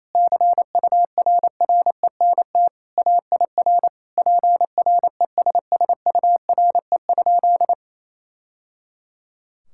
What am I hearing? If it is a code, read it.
Morse code, heard as a keyed tone, about 28 wpm, 700 Hz. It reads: CURRENT AIR PRESSURE?
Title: Sentences from Top 1000 Words - Review Part 6 - 28wpm